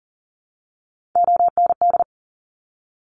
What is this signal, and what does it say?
Morse code, heard as a keyed tone, about 39 wpm, 700 Hz. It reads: OND
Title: Top Three Letter Combinations in Words 40wpm